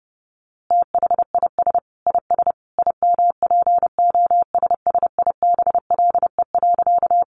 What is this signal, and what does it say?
Morse code, heard as a keyed tone, about 30 wpm, 700 Hz. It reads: THIS IS IMPOSSIBLE.
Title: A Sound Of Thunder 30wpm